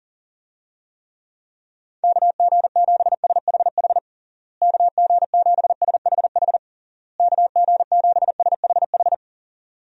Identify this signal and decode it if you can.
Morse code, heard as a keyed tone, about 40 wpm, 700 Hz. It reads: KG7SHH KG7SHH KG7SHH
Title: USA Call Signs - 6 Characters - Repeated 3x - 40wpm